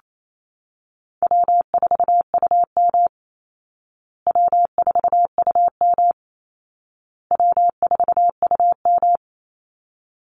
Morse code, heard as a keyed tone, about 28 wpm, 700 Hz. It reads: W4UM W4UM W4UM
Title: Easy Call Signs Repeated 3x - 28wpm